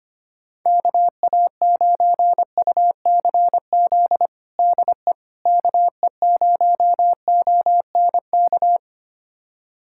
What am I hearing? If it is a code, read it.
Morse code, heard as a keyed tone, about 25 wpm, 700 Hz. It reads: KA9UCZ DE KE0ONK
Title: USA Callsign DE Callsign - 6 Characters - 25wpm